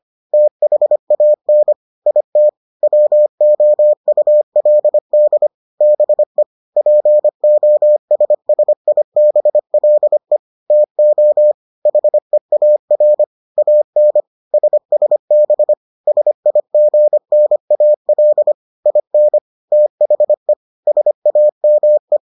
Morse code, heard as a keyed tone, about 25 wpm, 600 Hz. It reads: THAN IT WOULD BE POSSIBLE TO HEAR AN SSB SIGNAL IN THE SAME